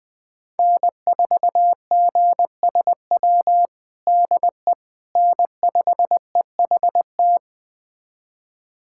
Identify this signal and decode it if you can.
Morse code, heard as a keyed tone, about 20 wpm, 700 Hz. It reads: N4GSW DE N5EHT
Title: USA Callsign DE Callsign - 5 Characters - 20wpm